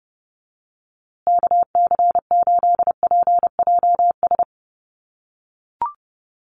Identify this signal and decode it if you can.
Morse code, heard as a keyed tone, about 29 wpm, 700 Hz. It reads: KC8PJS
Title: USA Call Signs Repeated 3x 30wpm